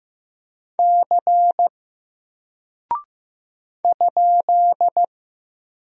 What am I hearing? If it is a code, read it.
Morse code, heard as a keyed tone, about 15 wpm, 700 Hz. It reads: C ?
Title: Single Character (Letter:Number:Punctuation:Prosign) 15wpm